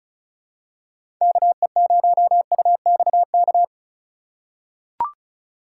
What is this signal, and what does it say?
Morse code, heard as a keyed tone, about 36 wpm, 700 Hz. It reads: KE0UXK
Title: USA Call Signs - 6 Characters - 2x Character Spacing - 35wpm